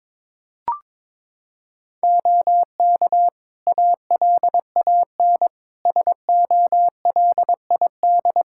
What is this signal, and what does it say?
Morse code, heard as a keyed tone, about 22 wpm, 700 Hz. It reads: OK ALAN SOLID
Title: Practice QSO - Exchange 3 - More Info - 22wpm